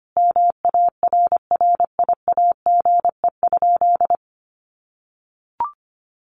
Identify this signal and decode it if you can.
Morse code, heard as a keyed tone, about 25 wpm, 700 Hz. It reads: MARRIAGE?
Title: Sentences from Top 1500 Words 25wpm